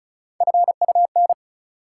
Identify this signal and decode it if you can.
Morse code, heard as a keyed tone, about 36 wpm, 700 Hz. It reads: FUN
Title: Sentences from Top 700 Words - New Word Drill - 35wpm